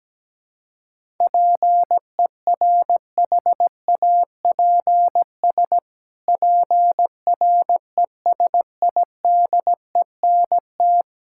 Morse code, heard as a keyed tone, about 17 wpm, 700 Hz. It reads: PERHAPS PRESIDENT
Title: Pangram Challenge v2 - 17wpm